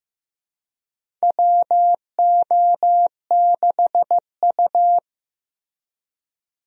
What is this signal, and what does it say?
Morse code, heard as a keyed tone, about 15 wpm, 700 Hz. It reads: WO6U
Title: USA Call Signs - 4 Characters - Repeated 3x - 15wpm